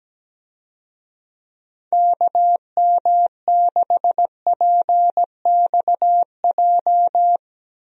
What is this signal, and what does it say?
Morse code, heard as a keyed tone, about 17 wpm, 700 Hz. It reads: KM6PXJ
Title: USA Call Signs - 6 Characters - Repeated 3x - 17wpm